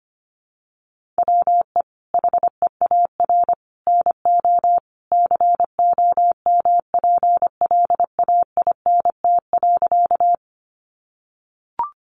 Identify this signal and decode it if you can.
Morse code, heard as a keyed tone, about 25 wpm, 700 Hz. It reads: WE HEAR NO COMPLAINT.
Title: Sentences from Top 1400 Words 25wpm